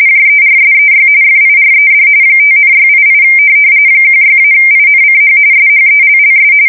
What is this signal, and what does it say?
RTTY, about 45.45 baud. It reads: JERSEY FIRED 5,688 16-INCH SHELLS
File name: fsk170 / USS-New-Jersey